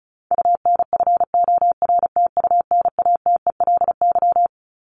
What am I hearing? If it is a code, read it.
Morse code, heard as a keyed tone, about 34 wpm, 700 Hz. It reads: UNFORTUNATELY
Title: Top 1751-2000 Words 35wpm